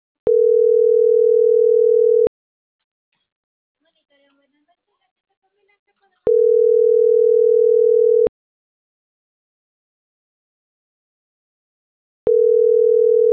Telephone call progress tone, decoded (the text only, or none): ringback tone